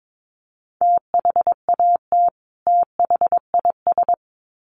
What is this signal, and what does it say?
Morse code, heard as a keyed tone, about 22 wpm, 700 Hz. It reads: THAT THIS
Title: Sets of 2 Words 22wpm